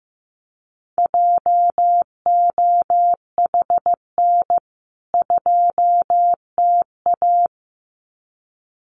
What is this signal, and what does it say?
Morse code, heard as a keyed tone, about 15 wpm, 700 Hz. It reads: JOHN 2TA
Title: CWT Exchange 15wpm